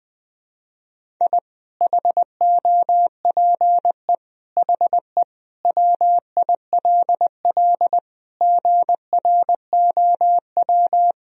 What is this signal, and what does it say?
Morse code, heard as a keyed tone, about 20 wpm, 700 Hz. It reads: I HOPE HE WILL GROW
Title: Sentences from Top 1000 Words - Review Part 1 - 20wpm